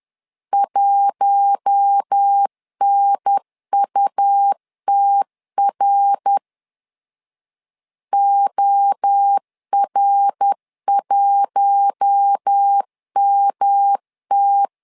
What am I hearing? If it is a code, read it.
Morse code, heard as a keyed tone, about 10 wpm, 800 Hz. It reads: 1NUTR OR1MT
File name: LEZIONI / 4 Q1 10wpm 2448